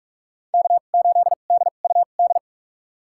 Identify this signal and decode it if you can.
Morse code, heard as a keyed tone, about 46 wpm, spaced as for 32 wpm, 700 Hz. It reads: K8DUD
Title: USA Call Signs - 5 Characters - 2x Character Spacing - 45wpm